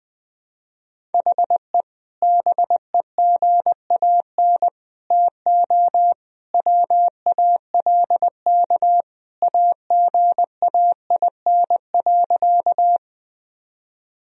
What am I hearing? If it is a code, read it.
Morse code, heard as a keyed tone, about 20 wpm, 700 Hz. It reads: HE BEGAN TO WALK AGAIN.
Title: Sentences from Top 1000 Words - Review Part 1 - 20wpm